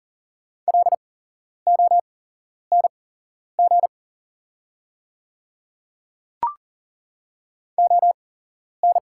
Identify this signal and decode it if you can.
Morse code, heard as a keyed tone, about 40 wpm, spaced as for 11 wpm, 690 Hz. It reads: LONG ON
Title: 8x Character Spacing - Top 100 Words - 40wpm